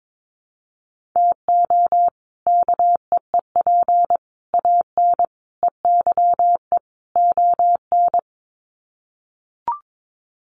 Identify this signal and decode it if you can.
Morse code, heard as a keyed tone, about 22 wpm, 700 Hz. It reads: TO KEEP AN EYE ON
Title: Sets of 5 Words Encore 22wpm